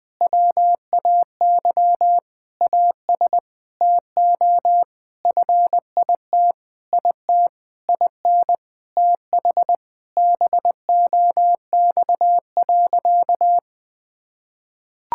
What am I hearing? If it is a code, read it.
Morse code, heard as a keyed tone, about 21 wpm, 700 Hz. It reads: WAY AS TO FIT IT IN TH BOX.
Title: Sentences with 6-Word Phrases - 20wpm